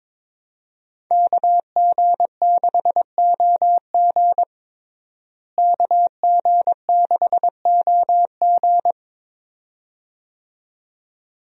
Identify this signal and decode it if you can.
Morse code, heard as a keyed tone, about 22 wpm, 700 Hz. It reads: KG6OG KG6OG
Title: USA Call Signs - 5 Characters - Repeated 2x - 22wpm